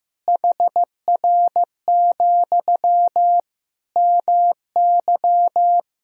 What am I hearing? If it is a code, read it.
Morse code, heard as a keyed tone, about 16 wpm, 700 Hz. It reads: HR, MY
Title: Sentences from Top 300 Words and Abbreviations 15wpm